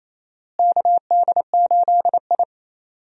Morse code, heard as a keyed tone, about 28 wpm, 700 Hz. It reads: KD8I
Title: USA Call Signs - 4 Characters - Repeated 2x - 28wpm